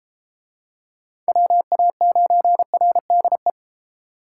Morse code, heard as a keyed tone, about 31 wpm, 700 Hz. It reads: WA9RDE